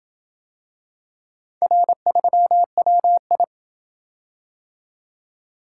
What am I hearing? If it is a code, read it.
Morse code, heard as a keyed tone, about 27 wpm, 700 Hz. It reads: R3WI